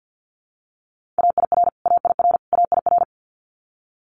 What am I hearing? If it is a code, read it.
Morse code, heard as a keyed tone, about 50 wpm, 700 Hz. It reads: AIR AIR AIR
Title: Multi-letter Phonemes Repeated 3x 50wpm